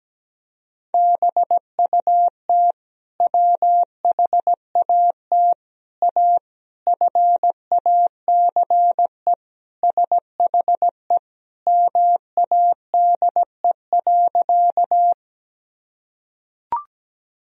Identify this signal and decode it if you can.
Morse code, heard as a keyed tone, about 17 wpm, 700 Hz. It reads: BUT WHAT A FACE SHE MADE.
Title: Sentences from Top 1000 Words - Review Part 2 - 17wpm